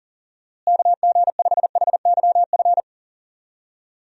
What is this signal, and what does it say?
Morse code, heard as a keyed tone, about 40 wpm, 700 Hz. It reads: KG5HYF
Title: USA Call Signs - 6 Characters - Repeated 3x - 40wpm